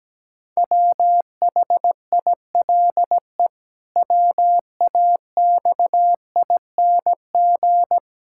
Morse code, heard as a keyed tone, about 17 wpm, 700 Hz. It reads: WHILE WAXING
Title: Pangram Challenge v2 - 17wpm